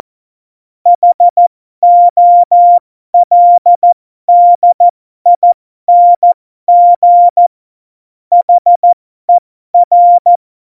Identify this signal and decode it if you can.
Morse code, heard as a keyed tone, about 14 wpm, spaced as for 12 wpm, 700 Hz. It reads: HOLDING HER
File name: Morsekurs / Witze